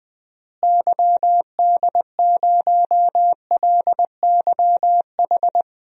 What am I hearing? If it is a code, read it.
Morse code, heard as a keyed tone, about 20 wpm, 700 Hz. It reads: YD0LYH
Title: DXCC-100 Call Signs - 6 Characters - Repeated 2x - 20wpm